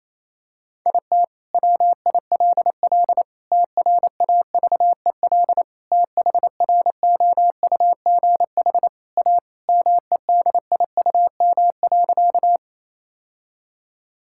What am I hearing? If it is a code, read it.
Morse code, heard as a keyed tone, about 27 wpm, 700 Hz. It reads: IT WILL TRAVEL THROUGH A MEDIUM.
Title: Sentences from Top 1000 Words - Review Part 6 - 28wpm